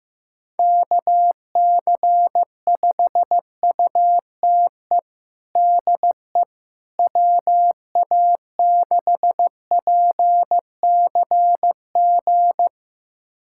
Morse code, heard as a keyed tone, about 15 wpm, 700 Hz. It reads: KC5UTE DE WA6PCG
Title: USA Callsign DE Callsign 15wpm